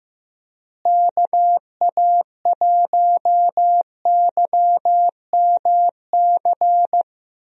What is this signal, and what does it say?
Morse code, heard as a keyed tone, about 15 wpm, 700 Hz. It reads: KA1YMC
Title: USA Call Signs - 6 Characters - Repeated 3x - 15wpm